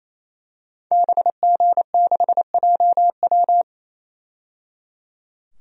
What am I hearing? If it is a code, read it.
Morse code, heard as a keyed tone, about 28 wpm, 700 Hz. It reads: BG6JW
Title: DXCC-100 Call Signs - 5 Characters - 28wpm